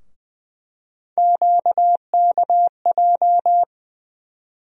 Morse code, heard as a keyed tone, about 20 wpm, 700 Hz. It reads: QKJ
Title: USA Call Sign Postfixes - 3 Letters 20wpm